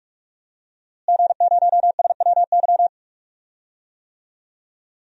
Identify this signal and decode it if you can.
Morse code, heard as a keyed tone, about 47 wpm, 700 Hz. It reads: G0SWY
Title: DXCC-100 Call Signs - 5 Characters - Repeated 2x - 45wpm